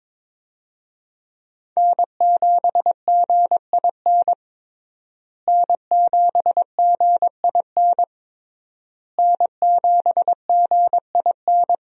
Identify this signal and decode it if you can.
Morse code, heard as a keyed tone, about 22 wpm, 700 Hz. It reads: N7GIN N7GIN N7GIN
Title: USA Call Signs - 5 Characters - Repeated 3x - 22wpm